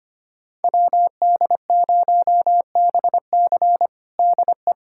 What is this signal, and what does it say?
Morse code, heard as a keyed tone, about 25 wpm, 700 Hz. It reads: WD0BC DE
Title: USA Callsign DE Callsign - 5 Characters - 25wpm